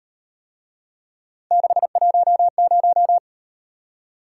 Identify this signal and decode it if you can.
Morse code, heard as a keyed tone, about 36 wpm, 700 Hz. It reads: 610